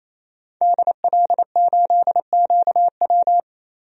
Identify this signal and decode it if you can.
Morse code, heard as a keyed tone, about 28 wpm, 700 Hz. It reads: DL8QW
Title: DXCC-100 Call Signs - 5 Characters - 28wpm